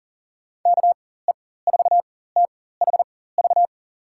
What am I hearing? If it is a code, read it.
Morse code, heard as a keyed tone, about 40 wpm, spaced as for 19 wpm, 700 Hz. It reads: KE4THV
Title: USA Call Signs - 6 Characters - 4x Character Spacing - 40wpm